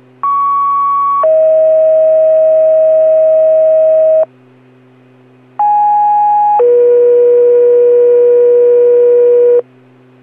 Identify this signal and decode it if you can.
Two-tone page, heard as a one-second tone, then a three-second tone, about 1120 then 640 Hz, about 830 then 480 Hz. It is heard from 0.2 to 9.6 s.